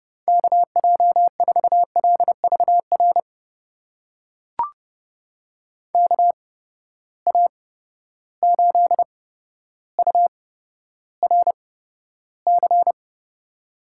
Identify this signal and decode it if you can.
Morse code, heard as a keyed tone, about 30 wpm, spaced as for 13 wpm, 700 Hz. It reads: KJ4LVR KA8URC
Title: USA Call Signs - 8x Character Spacing - 30wpm